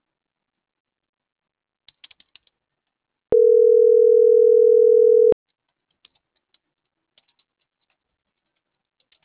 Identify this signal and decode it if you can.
Ringback tone (North America), 440 + 480 Hz.
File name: CENTRAL MAERSK / out gabrielle.melendez